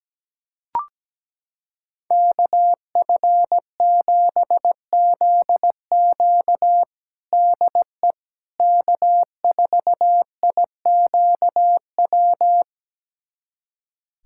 Morse code, heard as a keyed tone, about 17 wpm, 700 Hz. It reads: KF7ZQ DE K4IQW
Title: USA Callsign DE Callsign - 5 Characters - 17wpm